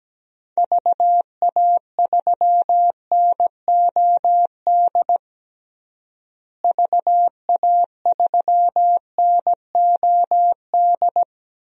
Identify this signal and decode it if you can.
Morse code, heard as a keyed tone, about 17 wpm, 700 Hz. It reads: VA3NOD VA3NOD
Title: DXCC-100 Call Signs - 6 Characters - Repeated 2x - 17wpm